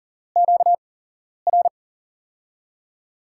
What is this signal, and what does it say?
Morse code, heard as a keyed tone, about 39 wpm, spaced as for 11 wpm, 700 Hz. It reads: QR
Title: USA Call Signs - 6 Characters - 8x Character Spacing - 40wpm